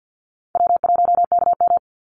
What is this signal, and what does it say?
Morse code, heard as a keyed tone, about 50 wpm, 700 Hz. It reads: W1KM
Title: Easy Call Signs - 50wpm